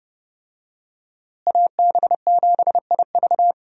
Morse code, heard as a keyed tone, about 30 wpm, 700 Hz. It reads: AB7IV
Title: USA Call Signs - 5 Characters - Repeated 3x - 30wpm